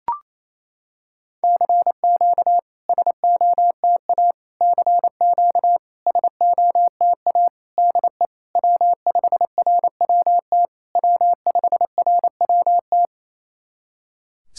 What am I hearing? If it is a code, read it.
Morse code, heard as a keyed tone, about 27 wpm, 690 Hz. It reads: CQ SOTA CQ SOTA DE W5RWT W5RWT
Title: SOTA QSO Practice 28wpm